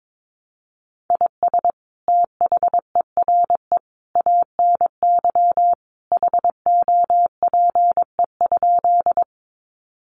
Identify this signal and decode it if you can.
Morse code, heard as a keyed tone, about 22 wpm, 700 Hz. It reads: IS THERE ANY HOPE?
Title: Sentences from Top 1000 Words 22wpm